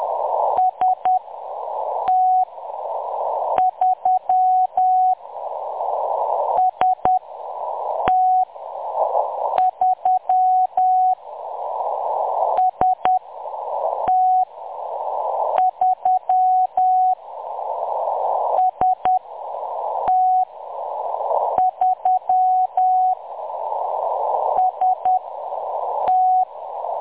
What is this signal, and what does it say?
Morse code, heard as a keyed tone, about 10 wpm, spaced as for 6 wpm, 740 Hz. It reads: ST3 ST3 ST3 ST3 ST
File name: M23-ST3- -0858utc-10184khz